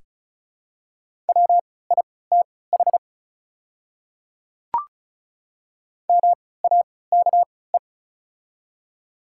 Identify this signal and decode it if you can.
Morse code, heard as a keyed tone, about 36 wpm, spaced as for 20 wpm, 690 Hz. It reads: WITH MAKE
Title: 3x Character Spacing - Top 100 Words - 35wpm